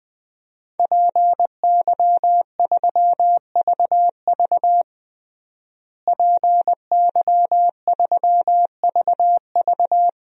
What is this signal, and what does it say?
Morse code, heard as a keyed tone, about 21 wpm, 700 Hz. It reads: PY3VV PY3VV
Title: DXCC-100 Call Signs - 5 Characters - Repeated 2x - 20wpm